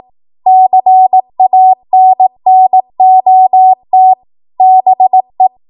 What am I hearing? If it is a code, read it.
Morse code, heard as a keyed tone, about 18 wpm, 750 Hz. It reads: CANNOT BE
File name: Morse / 18WPM